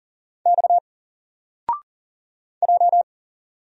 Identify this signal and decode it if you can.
Morse code, heard as a keyed tone, about 40 wpm, 700 Hz. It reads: X J
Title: Single Letter-Number Alt 40wpm